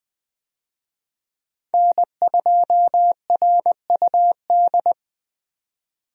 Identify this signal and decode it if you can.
Morse code, heard as a keyed tone, about 20 wpm, 700 Hz. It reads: N2RUD
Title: USA Call Signs - 5 Characters - 20wpm